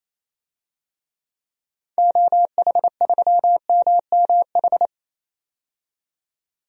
Morse code, heard as a keyed tone, about 28 wpm, 700 Hz. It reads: OH3MMH
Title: DXCC-100 Call Signs - 6 Characters - 28wpm